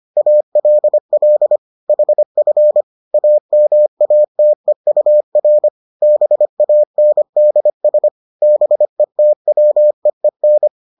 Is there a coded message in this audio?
Yes: Morse code from 0.2 to 10.7 s, heard as a keyed tone, about 25 wpm, 600 Hz. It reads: ALL HF AMATEUR BANDS BETWEEN